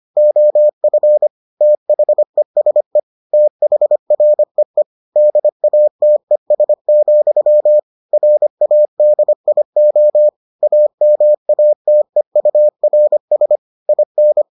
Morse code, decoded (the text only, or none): OF THESE THREE DATES, RADIO AMATEURS IN